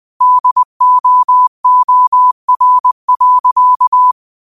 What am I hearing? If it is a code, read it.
Morse code, heard as a keyed tone, about 20 wpm, 1000 Hz. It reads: DOOR.
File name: qotdfiles / 20wpm